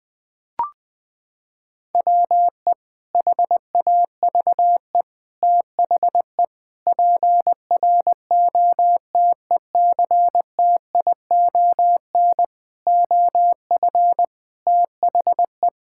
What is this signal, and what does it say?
Morse code, heard as a keyed tone, about 20 wpm, 700 Hz. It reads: WE HAVE THE PROTECTION OF THE
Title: Sentences from Top 1600 Words 20wpm